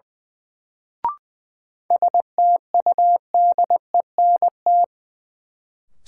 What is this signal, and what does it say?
Morse code, heard as a keyed tone, about 20 wpm, 700 Hz. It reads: STUDENT
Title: Sentences from Top 1000 Words - New Word Drill - 20wpm